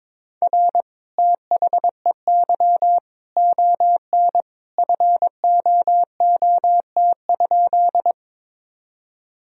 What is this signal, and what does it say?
Morse code, heard as a keyed tone, about 23 wpm, 700 Hz. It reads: R THEY ON FOOT?
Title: Sentences from Top 500 Words and Abbreviations 22wpm